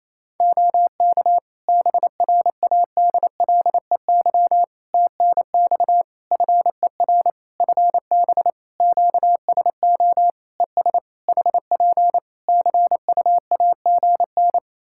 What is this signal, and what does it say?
Morse code, heard as a keyed tone, about 27 wpm, 700 Hz. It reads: OK BRADLEY TNX FER FB QSO ES HP CUAGN
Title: Practice QSO - Exchange 4 - Ending - 28wpm